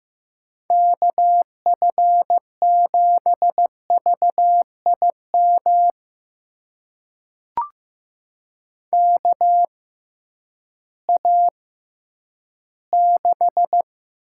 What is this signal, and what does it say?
Morse code, heard as a keyed tone, about 15 wpm, spaced as for 10 wpm, 700 Hz. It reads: KF7VIM KA6
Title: USA Call Signs - 6 Characters - 6x Character Spacing - 15wpm